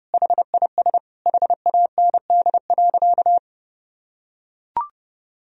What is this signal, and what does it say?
Morse code, heard as a keyed tone, about 30 wpm, 700 Hz. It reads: HIS HAND.
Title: Sentences from Top 1000 Words - Review Part 2 - 30wpm